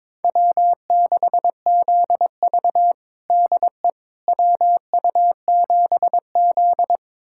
Morse code, heard as a keyed tone, about 22 wpm, 700 Hz. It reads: W6ZV DE WU7Z
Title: USA Callsign DE Callsign - 4 Characters - 22wpm